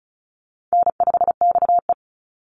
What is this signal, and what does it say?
Morse code, heard as a keyed tone, about 34 wpm, 700 Hz. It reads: N5XE
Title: Call Signs - 35wpm